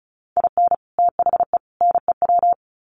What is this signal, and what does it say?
Morse code, heard as a keyed tone, about 34 wpm, 700 Hz. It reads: IN THE NEW
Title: Sets of 3 Words 35wpm